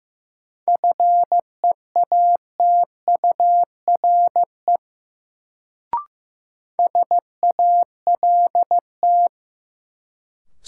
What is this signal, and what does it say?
Morse code, heard as a keyed tone, about 16 wpm, 700 Hz. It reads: FEATURE SALT
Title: Sentences from Top 800 Words - New Word Drill - 15wpm